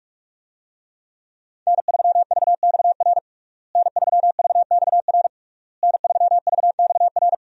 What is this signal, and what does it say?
Morse code, heard as a keyed tone, about 46 wpm, 700 Hz. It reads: N3VXR N3VXR N3VXR
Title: USA Call Signs - 5 Characters - Repeated 3x - 45wpm